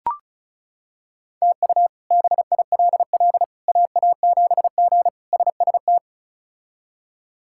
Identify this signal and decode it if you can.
Morse code, heard as a keyed tone, about 34 wpm, 700 Hz. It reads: TU BILL AA7G SST
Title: K1USN SST QSO Practice - 35wpm